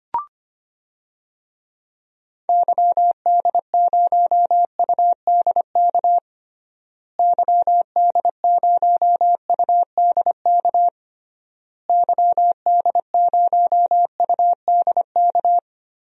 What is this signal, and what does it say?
Morse code, heard as a keyed tone, about 25 wpm, 700 Hz. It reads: YD0UDK YD0UDK YD0UDK
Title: DXCC-100 Call Signs - 6 Characters - Repeated 3x - 25wpm